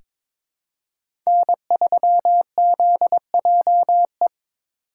Morse code, heard as a keyed tone, about 21 wpm, 700 Hz. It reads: N3ZJE
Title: USA Call Signs - 5 Characters - 8x Character Spacing - 22wpm